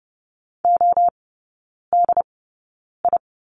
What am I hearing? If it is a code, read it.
Morse code, heard as a keyed tone, about 30 wpm, spaced as for 9 wpm, 700 Hz. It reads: ODI
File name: course / 30:10wpm / 087 Lesson 15e - Number 5 - Three Characters 30wpm